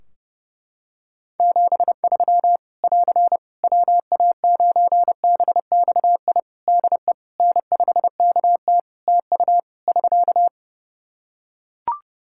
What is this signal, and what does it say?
Morse code, heard as a keyed tone, about 32 wpm, 700 Hz. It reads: 73 <AR> WA9BXI DE N5KT TU <SK>
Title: Practice QSO - Exchange 4 - Ending - 30wpm